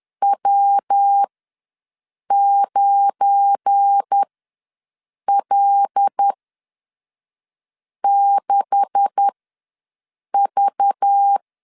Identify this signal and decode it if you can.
Morse code, heard as a keyed tone, about 10 wpm, spaced as for 5 wpm, 800 Hz. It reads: W9L6V
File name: LEZIONI / 9 2867 ws 10wpm 4215